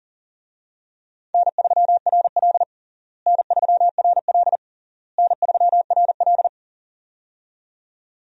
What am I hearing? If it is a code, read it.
Morse code, heard as a keyed tone, about 40 wpm, 700 Hz. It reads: N3RL N3RL N3RL
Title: USA Call Signs - 4 Characters - Repeated 3x - 40wpm